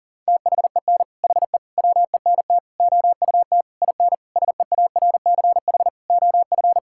Morse code, heard as a keyed tone, about 40 wpm, 700 Hz. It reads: THEN HE WENT OUT IN SEARCH OF
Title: Sentences from Top 1000 Words - Review Part 5 - 40wpm